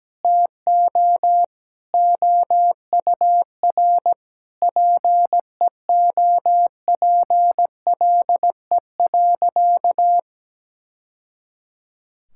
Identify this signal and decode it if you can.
Morse code, heard as a keyed tone, about 17 wpm, 700 Hz. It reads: TO OUR PEOPLE.
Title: Sentences from Top 1000 Words - Review Part 6 - 17wpm